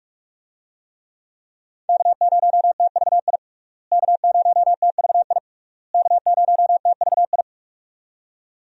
Morse code, heard as a keyed tone, about 46 wpm, 700 Hz. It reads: K0TVI K0TVI K0TVI
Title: USA Call Signs - 5 Characters - Repeated 3x - 45wpm